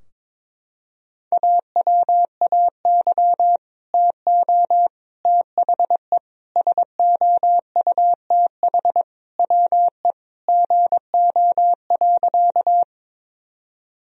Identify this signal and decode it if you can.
Morse code, heard as a keyed tone, about 22 wpm, 700 Hz. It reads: AWAY TO THE SOUTH WE GO.
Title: Sentences from Top 1000 Words - Review Part 2 - 22wpm